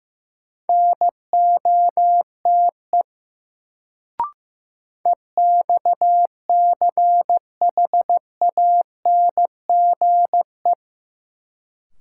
Morse code, heard as a keyed tone, about 15 wpm, 700 Hz. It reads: NOTE EXCHANGE
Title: Sets of 5 Words - New Word Drill - 15wpm